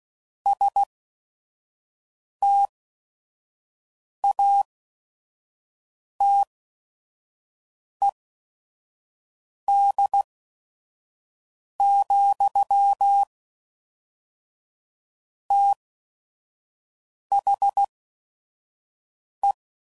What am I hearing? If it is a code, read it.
Morse code, heard as a keyed tone, about 16 wpm, spaced as for 5 wpm, 800 Hz. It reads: STATED, THE
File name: CW Code Groups @5WPM / Technical Subjects / XL